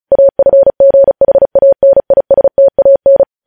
Morse code, heard as a keyed tone, about 36 wpm, 550 Hz. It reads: AFGHANISTAN
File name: podcast / News headlines-1x-35x35